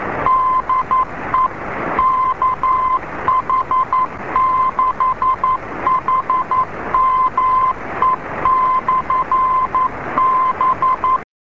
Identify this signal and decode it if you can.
Morse code, heard as a keyed tone, about 11 wpm, 1030 Hz, with noise. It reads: DE KH6HME/B